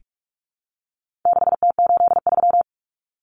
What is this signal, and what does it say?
Morse code, heard as a keyed tone, about 44 wpm, 700 Hz. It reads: 6T83
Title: Common Cut Numbers - 4-Digit 45wpm